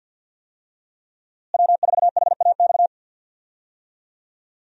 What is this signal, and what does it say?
Morse code, heard as a keyed tone, about 52 wpm, 700 Hz. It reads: W4HAX
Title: USA Call Signs - 5 Characters - Repeated 3x - 50wpm